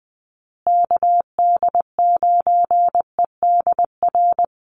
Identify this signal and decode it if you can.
Morse code, heard as a keyed tone, about 20 wpm, 700 Hz. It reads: KD9EDR
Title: USA Call Signs Repeated 2x 20wpm